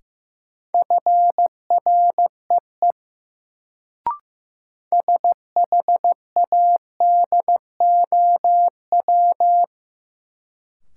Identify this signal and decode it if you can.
Morse code, heard as a keyed tone, about 15 wpm, 700 Hz. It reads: FREE SHADOW
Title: Sets of 5 Words Encore - New Word Drill - 15wpm